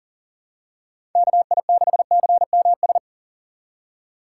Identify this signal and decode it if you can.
Morse code, heard as a keyed tone, about 40 wpm, 700 Hz. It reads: KI6CMS
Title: USA Call Signs - 6 Characters - 6x Character Spacing - 40wpm